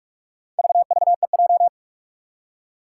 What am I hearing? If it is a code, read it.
Morse code, heard as a keyed tone, about 44 wpm, 700 Hz. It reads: VVEJ
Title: Four Challenging Letter-Number Combinations 45wpm